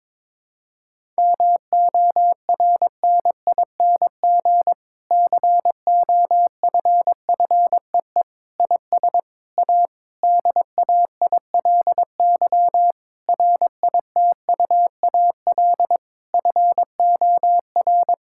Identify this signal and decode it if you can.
Morse code, heard as a keyed tone, about 23 wpm, 700 Hz. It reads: MORNING COFFEE IS A DAILY RITUAL FOR
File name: Difficult Sentences with Hints - 22wpm